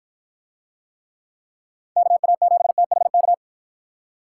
Morse code, heard as a keyed tone, about 53 wpm, 700 Hz. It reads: KA7THK